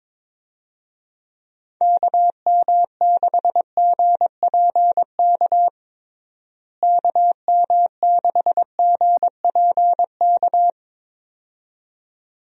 Morse code, heard as a keyed tone, about 22 wpm, 700 Hz. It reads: KM6GPK KM6GPK
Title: USA Call Signs - 6 Characters - Repeated 2x - 22wpm